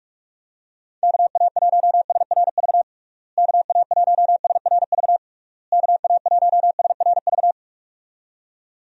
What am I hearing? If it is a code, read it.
Morse code, heard as a keyed tone, about 46 wpm, 700 Hz. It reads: KA1SRV KA1SRV KA1SRV
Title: USA Call Signs - 6 Characters - Repeated 3x - 45wpm